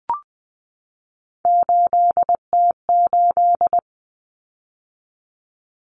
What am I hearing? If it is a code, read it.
Morse code, heard as a keyed tone, about 20 wpm, 700 Hz. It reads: 8T8
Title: Common Cut Numbers - 3-Digit 20wpm